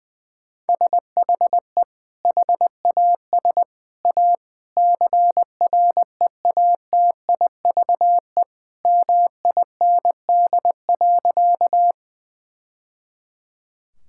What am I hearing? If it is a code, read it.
Morse code, heard as a keyed tone, about 21 wpm, 700 Hz. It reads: SHE HAS A CREATIVE MIND.
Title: Sentences from Top 1000 Words - Review Part 5 - 20wpm